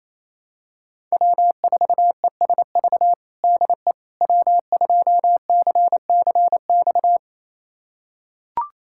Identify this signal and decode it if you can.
Morse code, heard as a keyed tone, about 28 wpm, 700 Hz. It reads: W4ESV DE W2CCX
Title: USA Callsign DE Callsign - 5 Characters - 28wpm